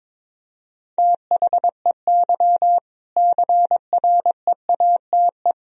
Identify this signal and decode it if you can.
Morse code, heard as a keyed tone, about 22 wpm, 700 Hz. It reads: THEY CREATE
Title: Sentences from Top 1000 Words - Review Part 3 - 22wpm